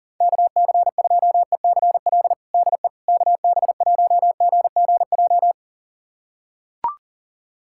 Morse code, heard as a keyed tone, about 40 wpm, 700 Hz. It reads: KC2ECL DE KB1GGJ
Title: USA Callsign DE Callsign - 6 Characters - 40wpm